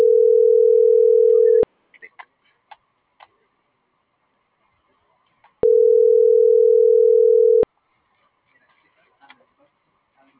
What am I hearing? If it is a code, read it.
Ringback tone (North America), 440 + 480 Hz.